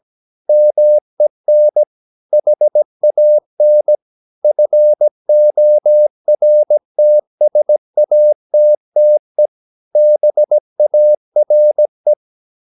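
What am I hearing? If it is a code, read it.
Morse code, heard as a keyed tone, about 17 wpm, 600 Hz. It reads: MEN HAN FORTSATTE BARE
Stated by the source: Slow (QRS) Morse practice at 17 wpm, from CW med Gnister.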